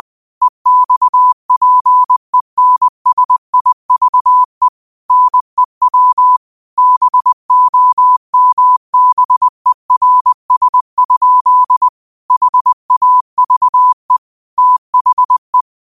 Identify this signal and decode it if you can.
Morse code, heard as a keyed tone, about 20 wpm, 1000 Hz. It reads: EXPENSIVE NEW BOMBERS? HAVE THE
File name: 20wpm